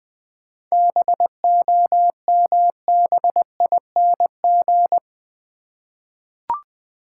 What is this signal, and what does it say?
Morse code, heard as a keyed tone, about 20 wpm, 700 Hz. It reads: BOMBING
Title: 7 Letter Words - First 3 Letters Given - 20wpm